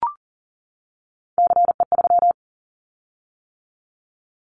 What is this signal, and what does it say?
Morse code, heard as a keyed tone, about 40 wpm, 690 Hz. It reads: CE3
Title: Callsign Prefixes for Top 100 DXCC - 40wpm